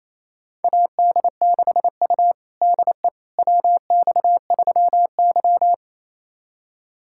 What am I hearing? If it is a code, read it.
Morse code, heard as a keyed tone, about 28 wpm, 700 Hz. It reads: AD6U DE WX3Y
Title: USA Callsign DE Callsign - 4 Characters - 28wpm